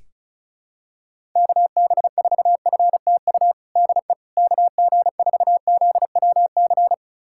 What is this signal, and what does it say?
Morse code, heard as a keyed tone, about 34 wpm, 700 Hz. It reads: KB4FTU DE KG4ZWC
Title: USA Callsign DE Callsign - 6 Characters - 35wpm